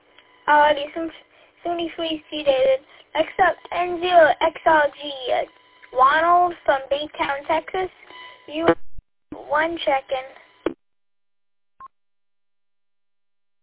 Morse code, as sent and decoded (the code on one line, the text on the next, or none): none